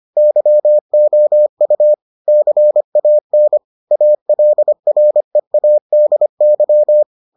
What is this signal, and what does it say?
Morse code, heard as a keyed tone, about 25 wpm, 600 Hz. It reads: YOU CAN ALREADY